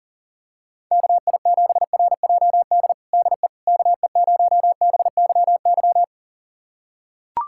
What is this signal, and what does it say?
Morse code, heard as a keyed tone, about 40 wpm, 700 Hz. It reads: KI7RJD DE KE0BYY
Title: USA Callsign DE Callsign - 6 Characters - 40wpm